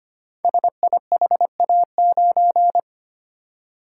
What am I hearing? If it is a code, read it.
Morse code, heard as a keyed tone, about 25 wpm, 700 Hz. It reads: SIHA9
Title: Code Groups - 5 Letter:Numbers - 25wpm